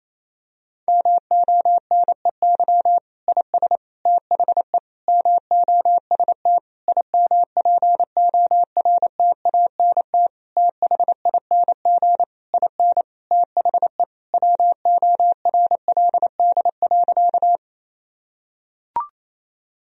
Morse code, heard as a keyed tone, about 28 wpm, 700 Hz. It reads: MONEY IS THE MOST IMPORTANT THING IN THE WORLD.
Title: Sentences from Top 1000 Words - Review Part 1 - 28wpm